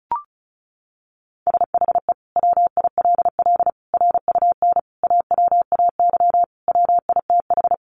Morse code, heard as a keyed tone, about 34 wpm, 700 Hz. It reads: SHE WILL RUN AWAY WITH
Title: Sentences from Top 600 Words 35wpm